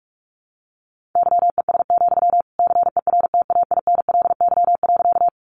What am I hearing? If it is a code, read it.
Morse code, heard as a keyed tone, about 44 wpm, 700 Hz. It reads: YES, CERTAINLY.
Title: Sentences from Top 900 Words 45wpm